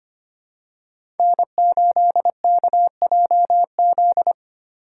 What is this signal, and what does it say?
Morse code, heard as a keyed tone, about 25 wpm, 700 Hz. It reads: N8KJZ
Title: Code Groups - 5 Letter:Numbers - 25wpm